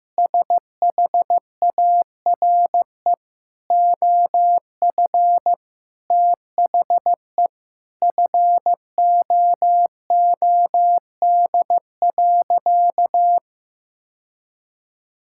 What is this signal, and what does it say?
Morse code, heard as a keyed tone, about 15 wpm, 700 Hz. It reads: SHARE OF THE FOOD.
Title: Sentences from Top 1000 Words - Review Part 6 - 15wpm